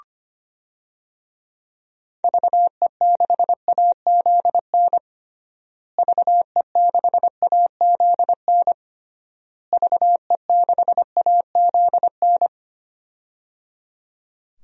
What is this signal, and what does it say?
Morse code, heard as a keyed tone, about 25 wpm, 700 Hz. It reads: VE6AZN VE6AZN VE6AZN
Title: DXCC-100 Call Signs - 6 Characters - Repeated 3x - 25wpm